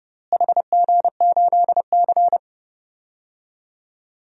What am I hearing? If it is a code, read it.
Morse code, heard as a keyed tone, about 31 wpm, 700 Hz. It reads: HG8C
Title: DXCC-100 Call Signs - 4 Characters - Repeated 3x - 30wpm